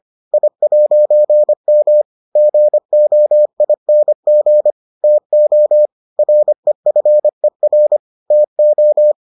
Morse code, heard as a keyed tone, about 25 wpm, 600 Hz. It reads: I'M GOING TO REFER TO